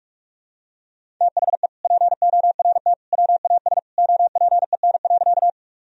Morse code, heard as a keyed tone, about 46 wpm, 700 Hz. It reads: THE PORT WAS OPEN.
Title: Sentences from Top 1000 Words - Review Part 1 - 45wpm